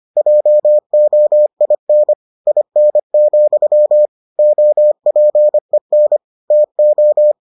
Morse code, heard as a keyed tone, about 25 wpm, 600 Hz. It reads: JOIN IN, OPEN TO